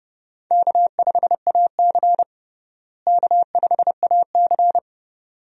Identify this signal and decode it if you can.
Morse code, heard as a keyed tone, about 30 wpm, 700 Hz. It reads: K5AC K5AC
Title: USA Call Signs - 4 Characters - Repeated 2x - 30wpm